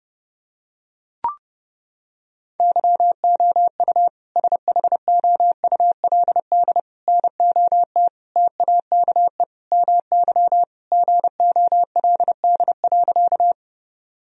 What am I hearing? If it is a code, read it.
Morse code, heard as a keyed tone, about 32 wpm, 700 Hz. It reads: YOU SHOULD NOT TAKE MY GOLD.
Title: Sentences from Top 1000 Words - Review Part 3 - 30wpm